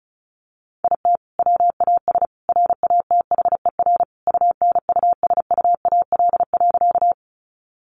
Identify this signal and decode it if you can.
Morse code, heard as a keyed tone, about 34 wpm, 700 Hz. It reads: IT WAS RATHER UNUSUAL.
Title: Sentences from Top 1400 Words 35wpm